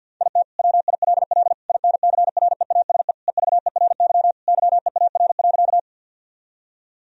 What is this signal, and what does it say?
Morse code, heard as a keyed tone, about 50 wpm, 700 Hz. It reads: IT WILL INCREASE EVERY YEAR.
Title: Sentences from Top 1000 Words - Review Part 4 - 50wpm